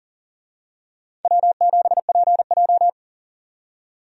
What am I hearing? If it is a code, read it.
Morse code, heard as a keyed tone, about 38 wpm, 700 Hz. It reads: W7PJ
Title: USA Call Signs - 4 Characters - 40wpm